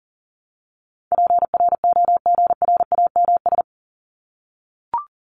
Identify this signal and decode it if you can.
Morse code, heard as a keyed tone, about 40 wpm, 700 Hz. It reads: PROGRAMS
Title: ES:S Suffix - Top 100 Words with Base Words 40wpm